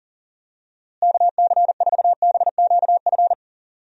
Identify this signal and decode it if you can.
Morse code, heard as a keyed tone, about 38 wpm, 700 Hz. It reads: KC4BQF
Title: USA Call Signs - 6 Characters - Repeated 3x - 40wpm